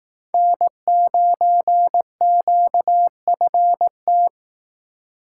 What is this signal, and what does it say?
Morse code, heard as a keyed tone, about 19 wpm, 700 Hz. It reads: N9QFT
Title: Speed Racing - USA 1x3 Call Signs 15wpm